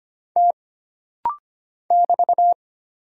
Morse code, heard as a keyed tone, about 25 wpm, 700 Hz. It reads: T <BT>
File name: Single Character (Letter:Number:Punctuation:Prosign) - Rapid-Fire - 25wpm